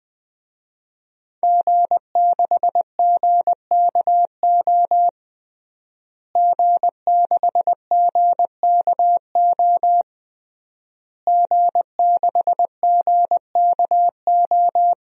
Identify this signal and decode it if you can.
Morse code, heard as a keyed tone, about 20 wpm, 700 Hz. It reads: G6GKO G6GKO G6GKO
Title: DXCC-100 Call Signs - 5 Characters - Repeated 3x - 20wpm